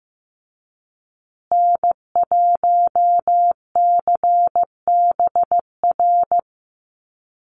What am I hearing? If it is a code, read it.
Morse code, heard as a keyed tone, about 14 wpm, 700 Hz. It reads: N1CBR
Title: USA Call Signs Repeated 3x 15wpm